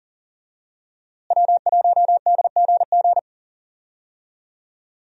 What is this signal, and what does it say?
Morse code, heard as a keyed tone, about 40 wpm, 700 Hz. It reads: W1DGG
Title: USA Call Signs - 5 Characters - Repeated 3x - 40wpm